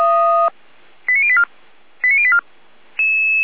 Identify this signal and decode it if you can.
Five-tone selective call (ZVEI), heard as quick runs of five tones, tones of about 70 ms. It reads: 89084 89084